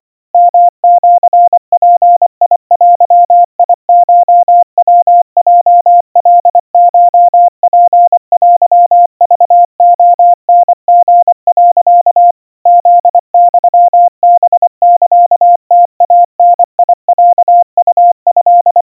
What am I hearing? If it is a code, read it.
Morse code (Japanese Wabun code), heard as a keyed tone, about 25 wpm, 700 Hz. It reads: ヨシヅデコヤヲカコツテクレタリ、 フユハサムイダロウト
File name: 90 cpm 700 Hz Crispy